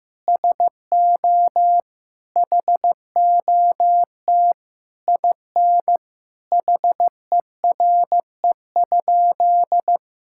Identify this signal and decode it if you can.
Morse code, heard as a keyed tone, about 15 wpm, 700 Hz. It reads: SO HOT IN HERE?
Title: Sentences from Top 1000 Words - Review Part 6 - 15wpm